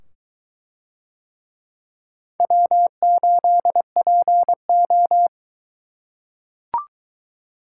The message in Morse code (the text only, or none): W8PO